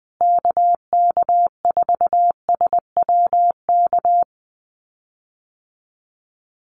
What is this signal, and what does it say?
Morse code, heard as a keyed tone, about 20 wpm, 700 Hz. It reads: KK4SWK
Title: USA Call Signs Repeated 2x 20wpm